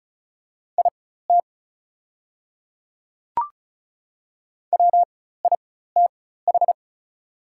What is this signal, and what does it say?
Morse code, heard as a keyed tone, about 36 wpm, spaced as for 16 wpm, 700 Hz. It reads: IT WITH
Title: 4x Character Spacing - Top 200 Words - 35wpm